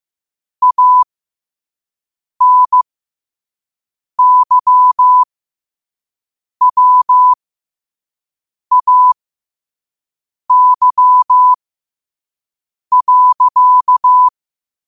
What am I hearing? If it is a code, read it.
Morse code, heard as a keyed tone, about 15 wpm, spaced as for 5 wpm, 1000 Hz. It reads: ANYWAY.
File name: qotdfiles / 5wpm